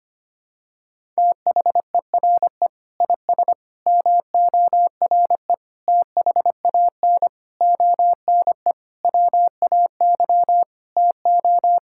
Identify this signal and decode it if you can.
Morse code, heard as a keyed tone, about 25 wpm, 700 Hz. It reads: THERE IS MORE THAN ONE WAY TO
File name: practice / 25wpm / Sentences with 9-Word Phrases - 25wpm